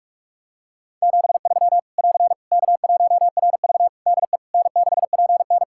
Morse code, heard as a keyed tone, about 46 wpm, 700 Hz. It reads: 73 <AR> K1RV DE N6PN
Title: Practice QSO - Exchange 4 - Ending - 45wpm